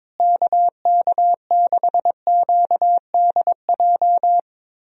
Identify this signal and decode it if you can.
Morse code, heard as a keyed tone, about 22 wpm, 700 Hz. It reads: KK6QDJ
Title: USA Call Signs - 6 Characters - Repeated 3x - 22wpm